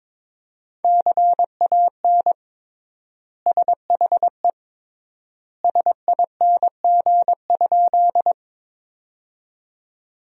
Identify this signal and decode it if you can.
Morse code, heard as a keyed tone, about 21 wpm, 700 Hz. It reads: CAN SHE SING?
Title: 3x Word-Spacing - Sentences from Top 400 Words - 22wpm